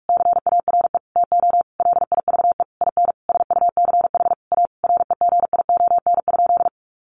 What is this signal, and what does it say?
Morse code, heard as a keyed tone, about 44 wpm, 700 Hz. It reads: CARE TO LIVE IN SUCH A REGION?
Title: Sentences from Top 1500 Words 45wpm